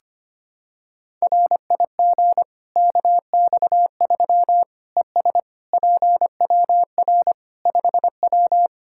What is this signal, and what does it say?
Morse code, heard as a keyed tone, about 25 wpm, 700 Hz. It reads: RIG KX3 ES PWR 5W
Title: Practice QSO - Exchange 2 - Info - 25wpm